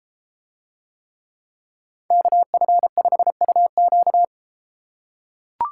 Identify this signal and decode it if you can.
Morse code, heard as a keyed tone, about 32 wpm, 700 Hz. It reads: KF5UQ